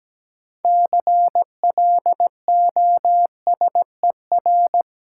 Morse code, heard as a keyed tone, about 17 wpm, 700 Hz. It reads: CLOSER
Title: Sets of 5 Words Encore - New Word Drill - 17wpm